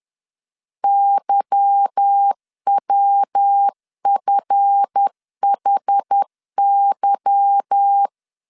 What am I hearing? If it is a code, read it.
Morse code, heard as a keyed tone, about 10 wpm, 800 Hz. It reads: YWFHY
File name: LEZIONI / 8 YJ45 / 8 Y 10wpm 2615